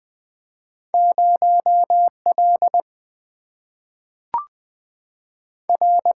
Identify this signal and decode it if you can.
Morse code, heard as a keyed tone, about 20 wpm, 700 Hz. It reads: 0 L R
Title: Single-Letter Single-Number 20wpm